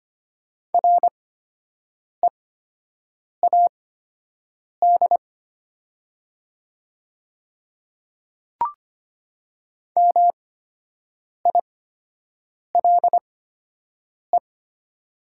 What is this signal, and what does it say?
Morse code, heard as a keyed tone, about 25 wpm, spaced as for 7 wpm, 700 Hz. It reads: READ MILE
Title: 8x Character Spacing - Top 300 Words - 25wpm